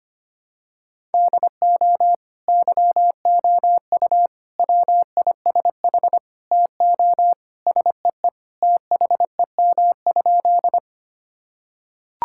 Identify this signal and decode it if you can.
Morse code, heard as a keyed tone, about 25 wpm, 700 Hz. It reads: DO YOU WISH TO SEE THEM?
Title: Sentences from Top 1600 Words 25wpm